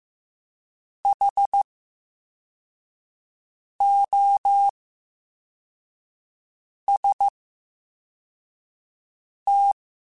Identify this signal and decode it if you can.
Morse code, heard as a keyed tone, about 15 wpm, 800 Hz. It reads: H O S T
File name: Special INGs-mp3s / Special INGs-mp3s / VerySlowToFaster Ing @13